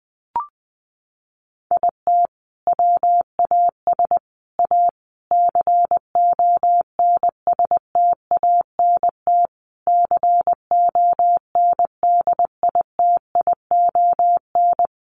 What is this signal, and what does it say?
Morse code, heard as a keyed tone, about 20 wpm, 700 Hz. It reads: IT WAS A CONSTANT CONDITION
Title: Sentences from Top 1500 Words 20wpm